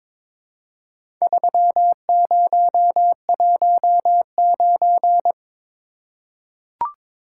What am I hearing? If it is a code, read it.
Morse code, heard as a keyed tone, about 22 wpm, 700 Hz. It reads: 3019
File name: practice / 22wpm / Four Letter-Number Combinations - 22wpm